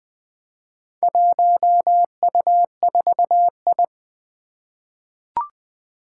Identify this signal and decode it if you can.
Morse code, heard as a keyed tone, about 21 wpm, 700 Hz. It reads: 1U4I
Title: Four Challenging Letter-Number Combinations 20wpm